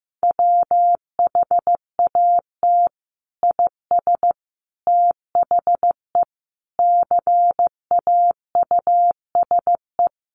Morse code, heard as a keyed tone, about 15 wpm, 700 Hz. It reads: WHAT IS THE CAUSE
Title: Sentences from Top 1100 Words 15wpm